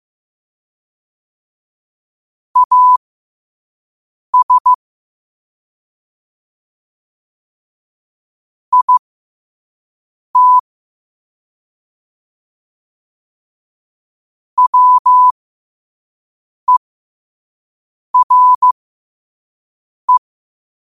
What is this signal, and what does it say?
Morse code, heard as a keyed tone, about 15 wpm, spaced as for 5 wpm, 1000 Hz. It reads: AS IT WERE
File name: qotdfiles / 5wpm